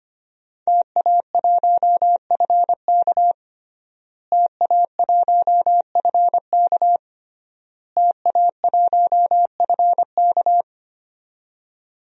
Morse code, heard as a keyed tone, about 25 wpm, 700 Hz. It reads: TA1FK TA1FK TA1FK
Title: DXCC-100 Call Signs - 5 Characters - Repeated 3x - 25wpm